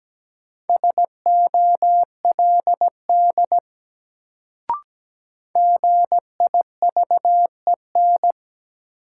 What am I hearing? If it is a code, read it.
Morse code, heard as a keyed tone, about 17 wpm, 700 Hz. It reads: SOLD GIVEN
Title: Sentences from Top 1400 Words - New Word Drill - 17wpm